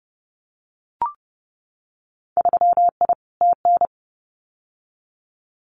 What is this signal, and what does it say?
Morse code, heard as a keyed tone, about 30 wpm, 700 Hz. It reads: 3I TN
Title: Winter Field Day Exchanges 30wpm